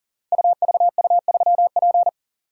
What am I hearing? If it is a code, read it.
Morse code, heard as a keyed tone, about 42 wpm, 700 Hz. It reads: UVU3P
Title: Code Groups - 5 Letter:Numbers - 40wpm